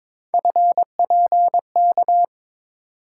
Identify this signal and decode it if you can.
Morse code, heard as a keyed tone, about 22 wpm, 700 Hz. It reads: FPK
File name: practice / 22wpm / Three Letter-Number Combinations - 22wpm